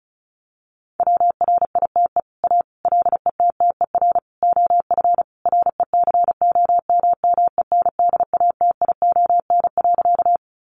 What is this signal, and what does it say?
Morse code, heard as a keyed tone, about 34 wpm, 700 Hz. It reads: WRITE A LETTER OF RECOMMENDATION.
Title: Sentences from Top 1400 Words 35wpm